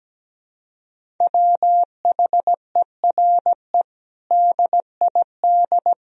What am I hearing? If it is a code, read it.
Morse code, heard as a keyed tone, about 17 wpm, 700 Hz. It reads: WHERE DID
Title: Sentences from Top 200 Words and Abbreviations 17wpm